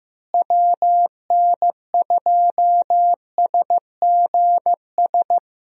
Morse code, heard as a keyed tone, about 15 wpm, 700 Hz. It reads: WN2SGS
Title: USA Call Signs - 6 Characters - Repeated 3x - 15wpm